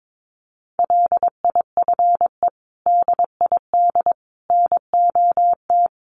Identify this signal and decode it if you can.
Morse code, heard as a keyed tone, about 22 wpm, 700 Hz. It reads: LIFE DID NOT
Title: Sentences from Top 800 Words 22wpm